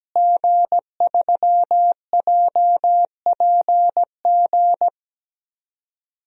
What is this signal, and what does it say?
Morse code, heard as a keyed tone, about 17 wpm, 700 Hz. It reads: G3JPG
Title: DXCC-100 Call Signs - 5 Characters - Repeated 2x - 17wpm